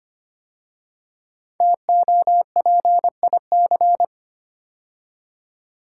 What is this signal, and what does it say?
Morse code, heard as a keyed tone, about 25 wpm, 700 Hz. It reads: TOPIC
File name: 5 Letter Words - First 3 Letters Given - 25wpm